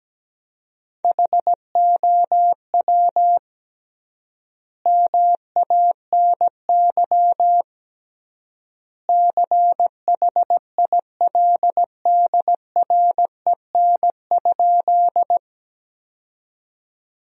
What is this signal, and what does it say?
Morse code, heard as a keyed tone, about 17 wpm, 700 Hz. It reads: HOW MANY CHILDREN?
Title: 3x Word-Spacing - Sentences from Top 300 Words - 17wpm